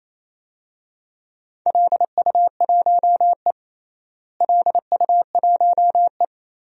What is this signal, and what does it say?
Morse code, heard as a keyed tone, about 28 wpm, 700 Hz. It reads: LU1E LU1E
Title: DXCC-100 Call Signs - 4 Characters - Repeated 3x - 28wpm